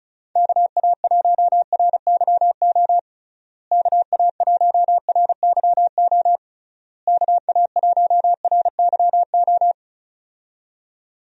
Morse code, heard as a keyed tone, about 35 wpm, 700 Hz. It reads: KA1RYO KA1RYO KA1RYO
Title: USA Call Signs - 6 Characters - Repeated 3x - 35wpm